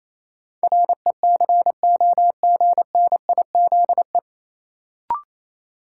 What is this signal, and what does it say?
Morse code, heard as a keyed tone, about 28 wpm, 700 Hz. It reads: RECOGNIZE
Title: Sentences from Top 1400 Words - New Word Drill Alt - 28wpm